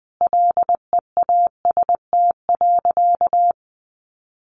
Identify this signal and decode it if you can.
Morse code, heard as a keyed tone, about 20 wpm, 700 Hz. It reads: LEAST.
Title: Sentences from Top 1300 Words 20wpm